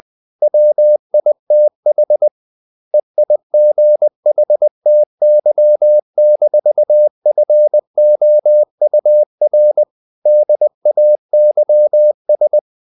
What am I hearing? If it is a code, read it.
Morse code, heard as a keyed tone, about 20 wpm, 600 Hz. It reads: WITH EIGHTY-FOUR DAYS
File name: the old man and the sea part 2 20wpm